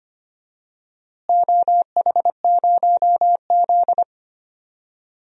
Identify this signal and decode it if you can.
Morse code, heard as a keyed tone, about 25 wpm, 700 Hz. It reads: OH0Z
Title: DXCC-100 Call Signs - 4 Characters - Repeated 2x - 25wpm